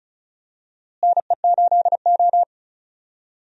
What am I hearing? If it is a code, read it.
Morse code, heard as a keyed tone, about 36 wpm, 700 Hz. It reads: NE8O
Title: USA Call Signs - 4 Characters - Repeated 2x - 35wpm